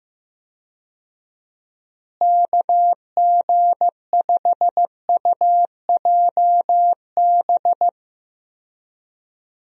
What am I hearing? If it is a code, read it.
Morse code, heard as a keyed tone, about 16 wpm, 700 Hz. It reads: KG5UJB
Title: USA Call Signs - 6 Characters - Repeated 3x - 15wpm